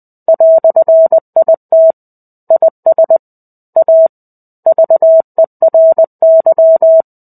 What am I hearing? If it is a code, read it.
Morse code, heard as a keyed tone, about 20 wpm, 650 Hz. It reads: "IT IS A VERY
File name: 05-Chapter-05 20 WPM